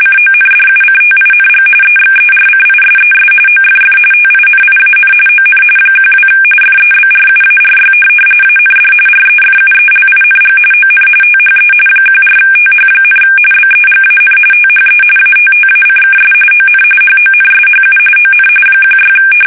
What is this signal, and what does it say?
RTTY, about 45.45 baud. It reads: CONDUCTING EXERCISES OFF TINIAN ON 12 FEBRUARY AND REFUELING ON THE 14TH, BEGAN THE ATTACKS ON HONSHU ON 16
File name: fsk850 / USS-Massachusetts